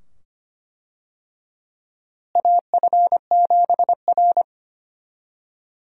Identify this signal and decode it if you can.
Morse code, heard as a keyed tone, about 25 wpm, 700 Hz. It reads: AF7R
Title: USA Call Signs - 4 Characters - Repeated 2x - 25wpm